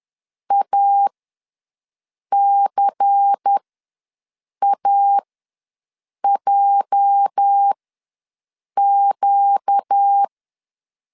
Morse code, heard as a keyed tone, about 10 wpm, spaced as for 6 wpm, 800 Hz. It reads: ACAJQ
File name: LEZIONI / 8 YJ45 / 8 YJ ws 10wpm 4043